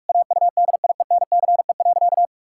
Morse code, heard as a keyed tone, about 44 wpm, 700 Hz. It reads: AUDIENCE.
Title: Pangram Challenge v2 - 45wpm